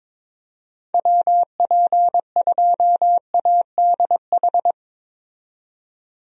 Morse code, heard as a keyed tone, about 22 wpm, 700 Hz. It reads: WP2ADH
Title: USA Call Signs - 6 Characters - 22wpm